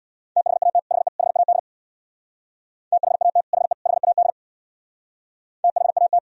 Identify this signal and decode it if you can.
Morse code, heard as a keyed tone, about 66 wpm, 700 Hz. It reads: THAT HE HAD THAT HE HAD THAT